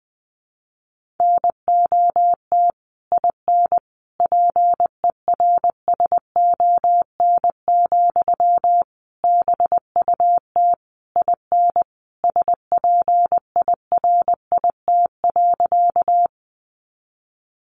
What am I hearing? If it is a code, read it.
Morse code, heard as a keyed tone, about 20 wpm, 700 Hz. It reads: NOT IN PERSON, BUT IN SPIRIT.
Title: Sentences from Top 1400 Words 20wpm